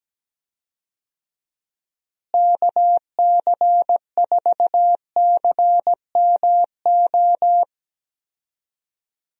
Morse code, heard as a keyed tone, about 17 wpm, 700 Hz. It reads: KC4CMO
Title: USA Call Signs - 6 Characters - Repeated 3x - 17wpm